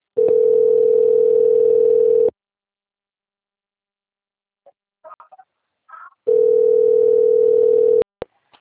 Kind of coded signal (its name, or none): telephone call progress tone